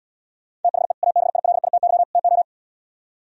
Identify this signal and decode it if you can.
Morse code, heard as a keyed tone, about 77 wpm, 700 Hz. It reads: THE ABILITY TO